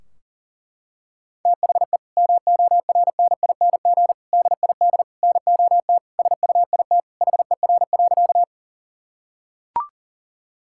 Morse code, heard as a keyed tone, about 40 wpm, 700 Hz. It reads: THE MORNING DID NOT SUIT HER.
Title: Sentences from Top 1600 Words 40wpm